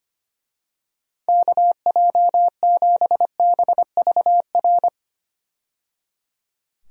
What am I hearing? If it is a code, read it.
Morse code, heard as a keyed tone, about 25 wpm, 700 Hz. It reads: KJ7BVR
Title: USA Call Signs - 6 Characters - 25wpm